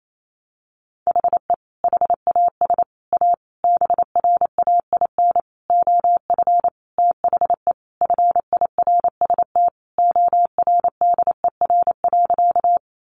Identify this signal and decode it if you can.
Morse code, heard as a keyed tone, about 28 wpm, 700 Hz. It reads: HE HAS A BRAIN OF THE FIRST ORDER.
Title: Sentences from Top 1100 Words 28wpm